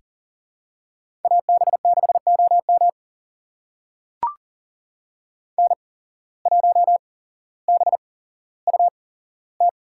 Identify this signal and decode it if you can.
Morse code, heard as a keyed tone, about 40 wpm, spaced as for 17 wpm, 700 Hz. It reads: AB6OM N1BUT
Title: USA Call Signs - 5 Characters - 8x Character Spacing - 40wpm